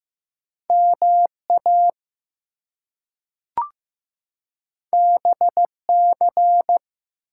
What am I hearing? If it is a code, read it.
Morse code, heard as a keyed tone, about 15 wpm, 700 Hz. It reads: MA BC
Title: US States and Canadian Provinces 15wpm